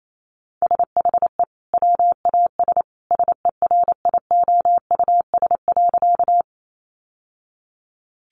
Morse code, heard as a keyed tone, about 28 wpm, 700 Hz. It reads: SHE WAS SERIOUS.
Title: Sentences from Top 1100 Words 28wpm